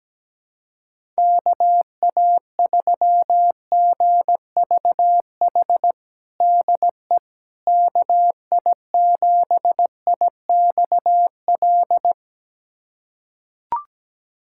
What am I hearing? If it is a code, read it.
Morse code, heard as a keyed tone, about 17 wpm, 700 Hz. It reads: KA3GVH DE KI7IXL
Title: USA Callsign DE Callsign - 6 Characters - 17wpm